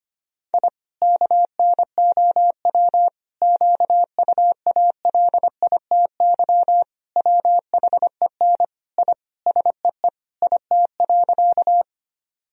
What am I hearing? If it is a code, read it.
Morse code, heard as a keyed tone, about 25 wpm, 700 Hz. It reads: I KNOW QUALITY WHEN I SEE IT.
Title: Sentences from Top 1000 Words - Review Part 4 - 25wpm